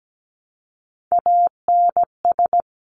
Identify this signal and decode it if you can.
Morse code, heard as a keyed tone, about 17 wpm, 700 Hz. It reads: ANS
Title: Top Three Letter Combinations in Words 17wpm